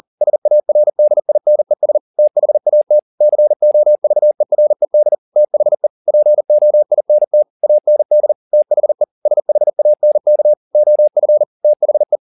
Morse code, heard as a keyed tone, about 40 wpm, 600 Hz. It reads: SARDINES THAT COVERED THE POINT AND THE SHANK OF THE